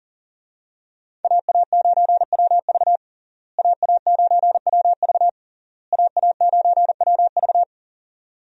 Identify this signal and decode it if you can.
Morse code, heard as a keyed tone, about 40 wpm, 700 Hz. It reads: AA9WV AA9WV AA9WV
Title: USA Call Signs - 5 Characters - Repeated 3x - 40wpm